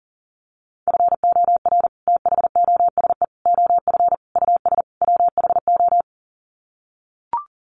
Morse code, heard as a keyed tone, about 40 wpm, 700 Hz. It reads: FOR THOSE OF US WHO
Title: Sets of 5 Words Encore 40wpm